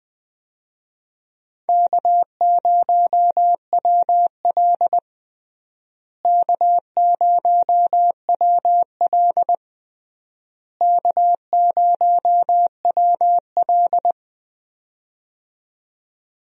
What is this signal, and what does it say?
Morse code, heard as a keyed tone, about 21 wpm, 700 Hz. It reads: K0WL K0WL K0WL
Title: USA Call Signs - 4 Characters - Repeated 3x - 20wpm